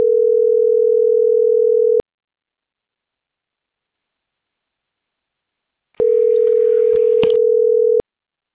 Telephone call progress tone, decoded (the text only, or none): ringback tone